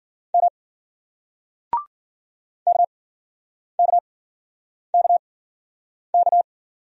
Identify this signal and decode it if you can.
Morse code, heard as a keyed tone, about 51 wpm, spaced as for 9 wpm, 700 Hz. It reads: M KKKK